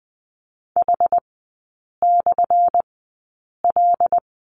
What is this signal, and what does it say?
Morse code, heard as a keyed tone, about 20 wpm, spaced as for 8 wpm, 700 Hz. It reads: H/L
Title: 178 Lesson 29e - Slash - Three Characters 20wpm